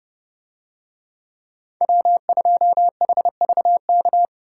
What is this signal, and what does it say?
Morse code, heard as a keyed tone, about 31 wpm, 700 Hz. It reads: W2HVK